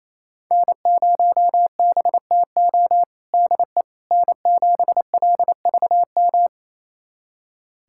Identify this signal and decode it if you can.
Morse code, heard as a keyed tone, about 28 wpm, 700 Hz. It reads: N0BTO DE N7LVM
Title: USA Callsign DE Callsign - 5 Characters - 28wpm